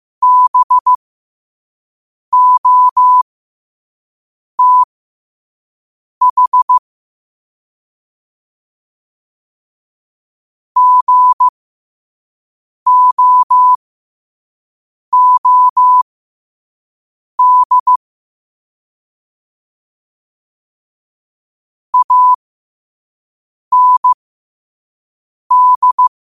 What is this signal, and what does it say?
Morse code, heard as a keyed tone, about 15 wpm, spaced as for 5 wpm, 1000 Hz. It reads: BOTH GOOD AND